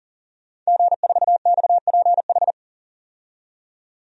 Morse code, heard as a keyed tone, about 40 wpm, 700 Hz. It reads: G4XPH
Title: DXCC-100 Call Signs - 5 Characters - 40wpm